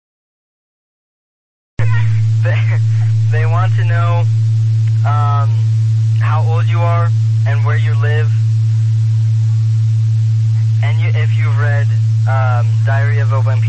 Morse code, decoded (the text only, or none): none